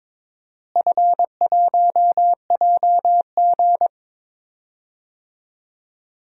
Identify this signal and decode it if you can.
Morse code, heard as a keyed tone, about 22 wpm, 700 Hz. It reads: F1JG
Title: DXCC-100 Call Signs - 4 Characters - Repeated 3x - 22wpm